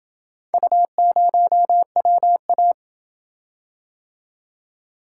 Morse code, heard as a keyed tone, about 27 wpm, 700 Hz. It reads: U0WA